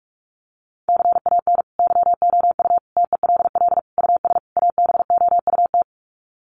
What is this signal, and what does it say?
Morse code, heard as a keyed tone, about 44 wpm, 700 Hz. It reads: CAN YOU TELL US ABOUT
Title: Sets of 5 Words Encore2 45wpm